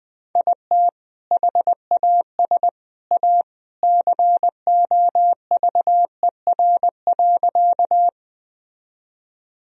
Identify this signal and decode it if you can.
Morse code, heard as a keyed tone, about 21 wpm, 700 Hz. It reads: IT HAS A COVER.
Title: Sentences from Top 1000 Words - Review Part 1 - 20wpm